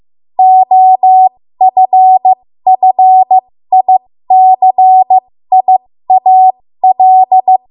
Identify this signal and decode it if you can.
Morse code, heard as a keyed tone, about 15 wpm, spaced as for 13 wpm, 750 Hz. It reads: OFFICIAL
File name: Morse / 13WPM